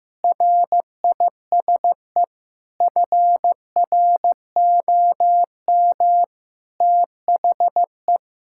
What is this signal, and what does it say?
Morse code, heard as a keyed tone, about 16 wpm, 700 Hz. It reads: RISE FROM THE
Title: Sentences from Top 1000 Words - Review Part 6 - 15wpm